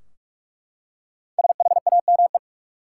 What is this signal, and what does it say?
Morse code, heard as a keyed tone, about 42 wpm, 700 Hz. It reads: SHAME
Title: Five Letter Words - 45wpm